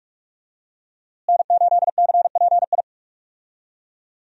Morse code, heard as a keyed tone, about 43 wpm, 700 Hz. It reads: N8CPI
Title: USA Call Signs - 5 Characters - Repeated 2x - 45wpm